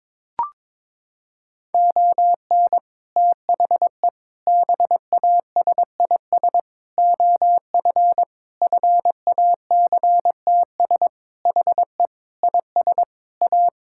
Morse code, heard as a keyed tone, about 22 wpm, 700 Hz. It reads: ON THE BASIS OF FACTS HE IS A
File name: practice / 22wpm / Sentences with 3-Word Phrases - 22wpm